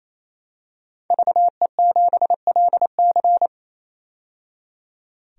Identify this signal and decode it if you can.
Morse code, heard as a keyed tone, about 28 wpm, 700 Hz. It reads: VE7LC
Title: DXCC-100 Call Signs - 5 Characters - 28wpm